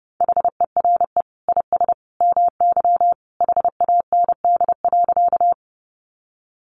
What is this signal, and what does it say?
Morse code, heard as a keyed tone, about 30 wpm, 700 Hz. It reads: HERE IS MY HAND.
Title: Sentences from Top 200 Words 30wpm